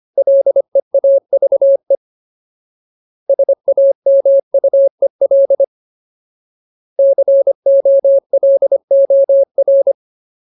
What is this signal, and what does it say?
Morse code, heard as a keyed tone, about 25 wpm, 550 Hz. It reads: LEAVE SAMUEL COLOR
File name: Ultimate Word List / 25WPM Ultimate